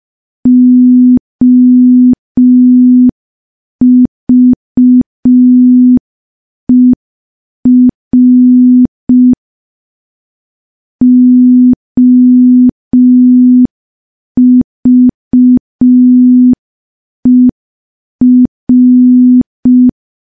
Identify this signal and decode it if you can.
Morse code, heard as a keyed tone, about 5 wpm, 250 Hz. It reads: OVER OVER
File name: morse / morse-training-05WPM